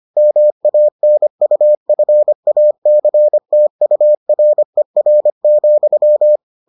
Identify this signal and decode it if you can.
Morse code, heard as a keyed tone, about 25 wpm, 600 Hz. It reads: MANUFACTURER,